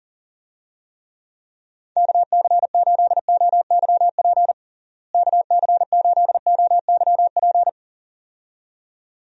Morse code, heard as a keyed tone, about 40 wpm, 700 Hz. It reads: KC8OYP KC8OYP
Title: USA Call Signs - 6 Characters - Repeated 2x - 40wpm